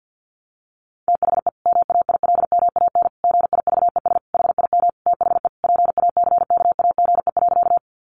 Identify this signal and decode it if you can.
Morse code, heard as a keyed tone, about 50 wpm, 700 Hz. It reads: THE MAILMAN GIVES HIM THE PACKAGE.
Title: Easy Phrases 50wpm